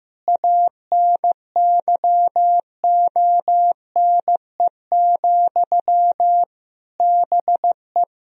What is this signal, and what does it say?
Morse code, heard as a keyed tone, about 15 wpm, 700 Hz. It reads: ANYONE, BE
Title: Sentences from Top 1000 Words - Review Part 1 - 15wpm